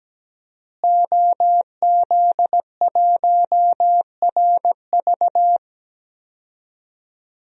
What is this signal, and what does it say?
Morse code, heard as a keyed tone, about 17 wpm, 700 Hz. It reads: OZ1RV
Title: DXCC-100 Call Signs - 5 Characters - Repeated 2x - 17wpm